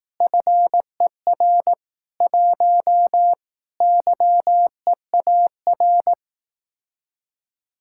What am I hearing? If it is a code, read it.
Morse code, heard as a keyed tone, about 18 wpm, 700 Hz. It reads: FER 1 YEAR
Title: Speed-Racing Common QSO Phrases Encore - 15wpm